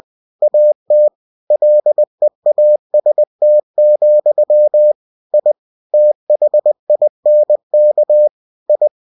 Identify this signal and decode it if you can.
Morse code, heard as a keyed tone, about 20 wpm, 600 Hz. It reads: AT LEAST, I THINK I
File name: cw-books / winnie-the-pooh / winnie-20wpm-page-0033